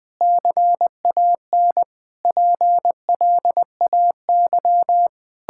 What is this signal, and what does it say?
Morse code, heard as a keyed tone, about 21 wpm, 700 Hz. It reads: CAN PLAY
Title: Sentences from Top 1000 Words - Review Part 5 - 20wpm